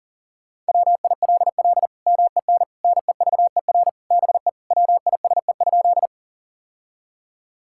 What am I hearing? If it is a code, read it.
Morse code, heard as a keyed tone, about 40 wpm, 700 Hz. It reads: WILL MEN NEVER BE WISE?
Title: Sentences from Top 1600 Words 40wpm